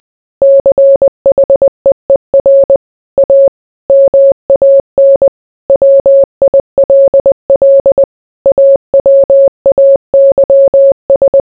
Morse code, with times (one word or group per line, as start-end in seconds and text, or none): 0.4-2.8 CHEER
3.2-3.5 A
3.9-5.3 MAN
5.7-8.0 WILL
8.5-11.4 AWAYS